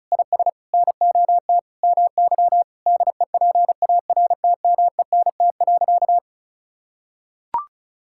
Morse code, heard as a keyed tone, about 36 wpm, 700 Hz. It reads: IS NOT MY DEPARTMENT.
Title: Sentences from Top 1000 Words - Review Part 5 - 35wpm